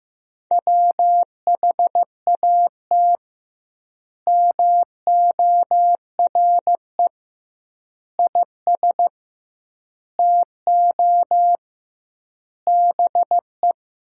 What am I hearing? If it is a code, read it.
Morse code, heard as a keyed tone, about 15 wpm, 700 Hz. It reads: WHAT MORE IS TO BE
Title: 2x Word-Spacing - Sentences from Top 500 Words - 15wpm